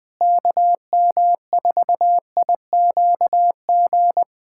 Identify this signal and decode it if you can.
Morse code, heard as a keyed tone, about 21 wpm, 700 Hz. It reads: KM4IQG
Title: USA Callsign DE Callsign - 6 Characters - 20wpm